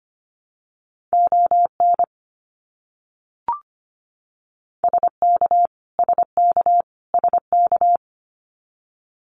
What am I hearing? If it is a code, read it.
Morse code, heard as a keyed tone, about 25 wpm, 700 Hz. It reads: ON SK SK SK
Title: Canadian Provinces and Territories - Repeated 3x - 25wpm